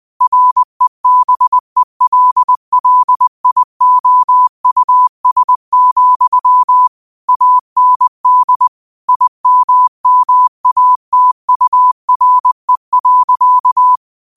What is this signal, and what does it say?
Morse code, heard as a keyed tone, about 20 wpm, 1000 Hz. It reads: REBELLIOUS, AND IMMATURE.
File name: 20wpm